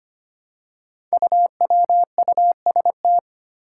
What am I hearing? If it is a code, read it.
Morse code, heard as a keyed tone, about 25 wpm, 700 Hz. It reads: UWUST
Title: Five Challenging Letter-Number Combinations 25wpm